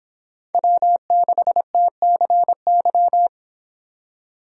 Morse code, heard as a keyed tone, about 30 wpm, 700 Hz. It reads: W6TCY